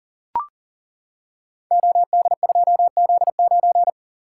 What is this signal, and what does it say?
Morse code, heard as a keyed tone, about 41 wpm, 690 Hz. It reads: OD2Z9
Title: Code Groups - 5 Letter:Numbers - 40wpm